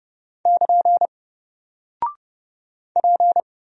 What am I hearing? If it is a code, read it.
Morse code, heard as a keyed tone, about 30 wpm, 700 Hz. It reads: <KN> P
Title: Single Character (Letter:Number:Punctuation:Prosign) 30wpm